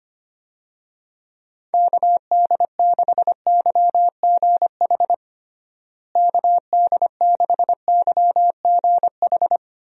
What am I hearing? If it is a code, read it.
Morse code, heard as a keyed tone, about 25 wpm, 700 Hz. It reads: KD6YGH KD6YGH
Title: USA Call Signs - 6 Characters - Repeated 2x - 25wpm